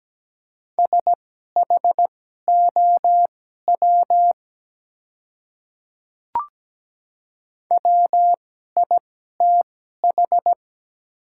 Morse code, heard as a keyed tone, about 17 wpm, spaced as for 12 wpm, 700 Hz. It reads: SHOW WITH
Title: 2x Character Spacing - Top 200 Words - 17wpm